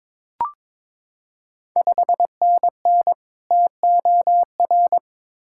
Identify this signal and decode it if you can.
Morse code, heard as a keyed tone, about 22 wpm, 690 Hz. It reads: 5NN TOR
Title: Ontario QSO Party Exchange 22wpm